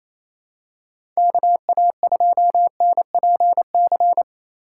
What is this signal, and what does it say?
Morse code, heard as a keyed tone, about 28 wpm, 700 Hz. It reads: KA2NPC
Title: USA Call Signs - 6 Characters - 4x Character Spacing - 28wpm